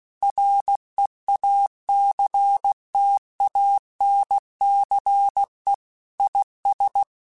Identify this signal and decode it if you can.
Morse code, heard as a keyed tone, about 16 wpm, 800 Hz. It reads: REACTANCE IS
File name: Technical Subjects / Technical Subjects-mp3s / XL @15